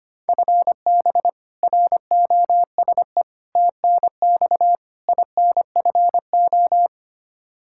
Morse code, heard as a keyed tone, about 25 wpm, 700 Hz. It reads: FB ROSE TNX INFO
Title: Practice QSO - Exchange 3 - More Info - 25wpm